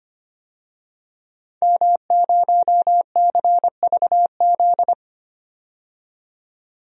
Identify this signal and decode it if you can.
Morse code, heard as a keyed tone, about 25 wpm, 700 Hz. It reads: M0CVZ
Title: DXCC-100 Call Signs - 5 Characters - 25wpm